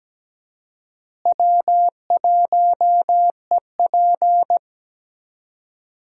Morse code, heard as a keyed tone, about 17 wpm, 700 Hz. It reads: W1EP
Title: USA Call Signs - 4 Characters - Repeated 3x - 17wpm